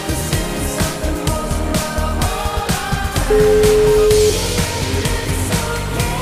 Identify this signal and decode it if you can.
Ringback tone (Europe, 425 Hz), with noise.